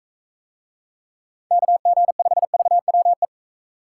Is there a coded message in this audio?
Yes: Morse code from 1.5 to 3.2 s, heard as a keyed tone, about 43 wpm, 700 Hz. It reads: KG5VWE